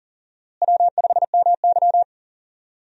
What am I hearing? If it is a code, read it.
Morse code, heard as a keyed tone, about 39 wpm, 700 Hz. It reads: W5MY
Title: USA Call Signs - 4 Characters - Repeated 3x - 40wpm